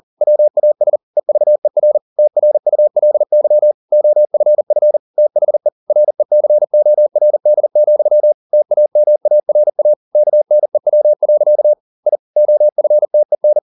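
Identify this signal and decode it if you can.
Morse code, heard as a keyed tone, about 40 wpm, 600 Hz. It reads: WAS EVER TRULY OFF THE RECORD, TAMARA KNEW. I OFTEN
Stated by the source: Morse code at 40 wpm, practice text, CW med Gnister.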